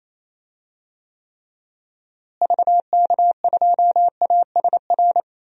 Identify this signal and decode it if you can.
Morse code, heard as a keyed tone, about 28 wpm, 700 Hz. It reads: VK2ASR
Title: DXCC-100 Call Signs - 6 Characters - Repeated 2x - 28wpm